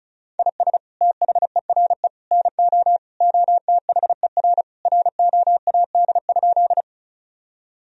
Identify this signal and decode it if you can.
Morse code, heard as a keyed tone, about 36 wpm, 700 Hz. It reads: IS THERE NO OTHER ROAD?
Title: Sentences from Top 1000 Words - Review Part 2 - 35wpm